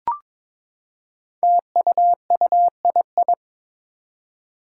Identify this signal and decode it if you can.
Morse code, heard as a keyed tone, about 22 wpm, 700 Hz. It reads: TUUII
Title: Five Challenging Letter-Number Combinations 22wpm